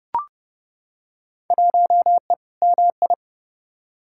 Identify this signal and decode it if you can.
Morse code, heard as a keyed tone, about 31 wpm, 700 Hz. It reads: 1E MI
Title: Field Day Exchanges - 30wpm